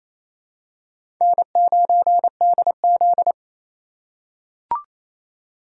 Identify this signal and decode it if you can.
Morse code, heard as a keyed tone, about 27 wpm, 700 Hz. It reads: N9DZ
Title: USA Call Signs - 4 Characters - 4x Character Spacing - 28wpm